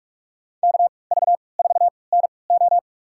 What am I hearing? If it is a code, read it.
Morse code, heard as a keyed tone, about 46 wpm, spaced as for 25 wpm, 700 Hz. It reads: KV4NO
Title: USA Call Signs - 5 Characters - 3x Character Spacing - 45wpm